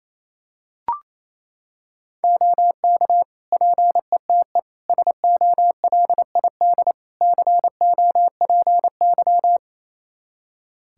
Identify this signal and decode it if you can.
Morse code, heard as a keyed tone, about 27 wpm, 700 Hz. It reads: OK PETE SOLID COPY
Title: Practice QSO - Exchange 3 - More Info - 28wpm